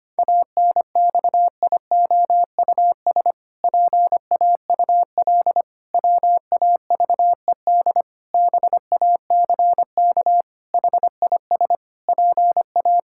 Morse code, heard as a keyed tone, about 25 wpm, 700 Hz. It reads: ANXIOUS PAUL WAVED BACK HIS PA
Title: Pangram Challenge v2 - 25wpm